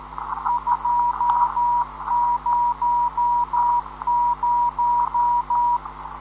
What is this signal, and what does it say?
Morse code, heard as a keyed tone, about 13 wpm, 960 Hz, with radio noise. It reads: 200